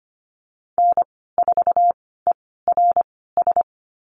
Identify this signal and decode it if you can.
Morse code, heard as a keyed tone, about 25 wpm, spaced as for 16 wpm, 690 Hz. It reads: N4ERS
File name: Lesson 15g - Number 5 - Call Signs 25wpm